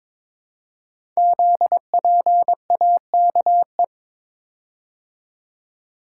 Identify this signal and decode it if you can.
Morse code, heard as a keyed tone, about 22 wpm, 700 Hz. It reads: ZPAKE
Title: Code Groups - 5 Letter:Numbers - 22wpm